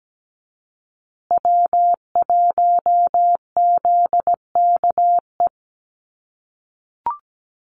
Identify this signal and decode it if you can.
Morse code, heard as a keyed tone, about 17 wpm, 700 Hz. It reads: W1ZKE
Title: USA Call Signs Repeated 3x 17wpm